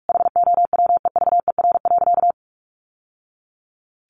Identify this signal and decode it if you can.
Morse code, heard as a keyed tone, about 44 wpm, 700 Hz. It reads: HOWEVER.
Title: Sentences from Top 600 Words 45wpm